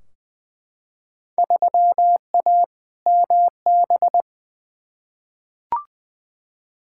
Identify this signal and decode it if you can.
Morse code, heard as a keyed tone, about 20 wpm, 700 Hz. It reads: 3A MB
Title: Field Day Exchanges Encore - 20wpm